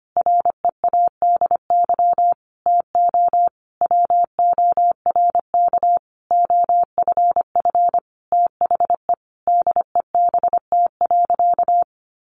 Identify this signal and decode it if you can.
Morse code, heard as a keyed tone, about 25 wpm, 700 Hz. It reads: READY TO WORK OFF THE DEBT.
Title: Sentences from Top 1300 Words 25wpm